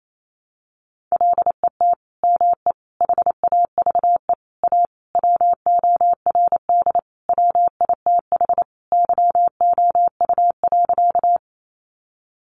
Morse code, heard as a keyed tone, about 28 wpm, 700 Hz. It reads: LET ME HAVE A WORD WITH YOU.
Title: Sentences from Top 300 Words 28wpm